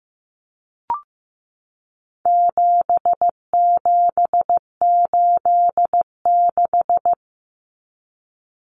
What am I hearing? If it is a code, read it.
Morse code, heard as a keyed tone, about 15 wpm, 700 Hz. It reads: 7786
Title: Numbers - 4-Digit 15wpm